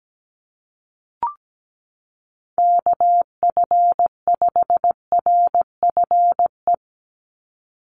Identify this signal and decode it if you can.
Morse code, heard as a keyed tone, about 17 wpm, 700 Hz. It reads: KF5RFE
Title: USA Call Signs 17wpm